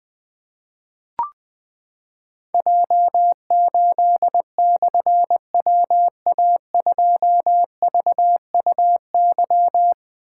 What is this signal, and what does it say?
Morse code, heard as a keyed tone, about 20 wpm, 700 Hz. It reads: J8/WA2VUY
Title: Difficult Call Signs 20wpm